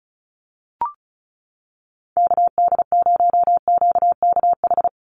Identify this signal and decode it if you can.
Morse code, heard as a keyed tone, about 34 wpm, 700 Hz. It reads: KD0QKH
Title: USA Call Signs 35wpm